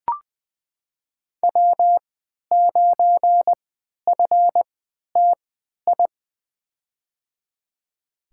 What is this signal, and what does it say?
Morse code, heard as a keyed tone, about 20 wpm, spaced as for 11 wpm, 700 Hz. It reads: W9FTI
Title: USA Call Signs - 3x Character Spacing - 20wpm